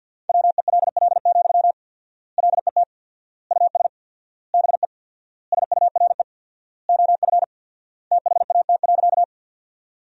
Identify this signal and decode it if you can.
Morse code, heard as a keyed tone, about 47 wpm, 700 Hz. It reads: WELL, LET US BE SURE OF THAT.
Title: 4x Word-Spacing - Sentences from Top 300 Words - 50wpm